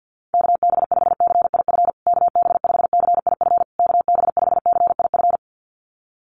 Morse code, heard as a keyed tone, about 50 wpm, 700 Hz. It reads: KB5CIF KB5CIF KB5CIF
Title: USA Call Signs Repeated 3x 50wpm